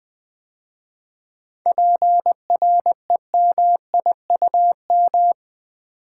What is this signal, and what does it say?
Morse code, heard as a keyed tone, about 20 wpm, 700 Hz. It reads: PREMIUM
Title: 7 Letter Words - First 3 Letters Given - 20wpm